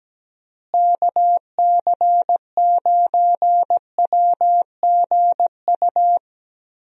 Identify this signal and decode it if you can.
Morse code, heard as a keyed tone, about 17 wpm, 700 Hz. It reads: KC9WGU
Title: USA Call Signs - 6 Characters - Repeated 3x - 17wpm